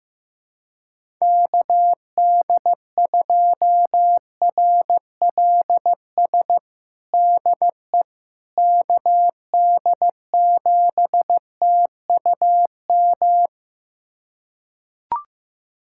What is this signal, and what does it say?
Morse code, heard as a keyed tone, about 15 wpm, 700 Hz. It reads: KD2RLS DE KD7TUM
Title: USA Callsign DE Callsign - 6 Characters - 15wpm